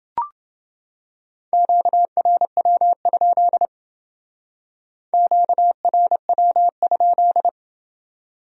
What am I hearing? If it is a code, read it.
Morse code, heard as a keyed tone, about 28 wpm, 700 Hz. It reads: QRW? QRW?